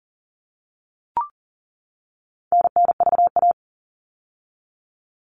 Morse code, heard as a keyed tone, about 40 wpm, 700 Hz. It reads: NNVA
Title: All Cut Numbers - 4-Digit 40wpm